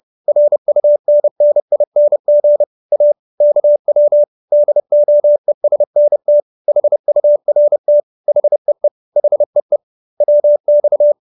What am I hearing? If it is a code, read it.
Morse code, heard as a keyed tone, about 30 wpm, 590 Hz. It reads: RUNNING A KW DOESNT HURT HEE HEE WX
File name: QSO321 30